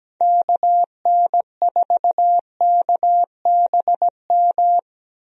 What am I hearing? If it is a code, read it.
Morse code, heard as a keyed tone, about 17 wpm, 700 Hz. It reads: KN4KBM
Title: USA Call Signs - 6 Characters - 17wpm